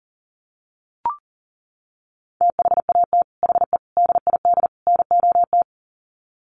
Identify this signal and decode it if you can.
Morse code, heard as a keyed tone, about 40 wpm, 700 Hz. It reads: THAT HE DID NOT
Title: Sets of 4 Words 40wpm